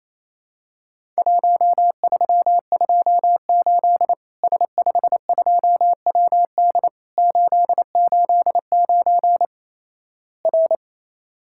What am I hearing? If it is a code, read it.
Morse code, heard as a keyed tone, about 28 wpm, 700 Hz. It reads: 1328 S52WD 889 R
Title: Worked All Europe DX Contest - QTC Exchange - 28wpm